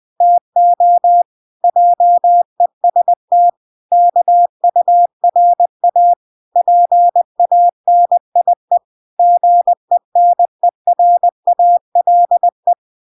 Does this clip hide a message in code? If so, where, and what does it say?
Yes: Morse code from 0.2 to 12.7 s, heard as a keyed tone, about 20 wpm, 700 Hz. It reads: TO JEST KURA PANIE GENERALE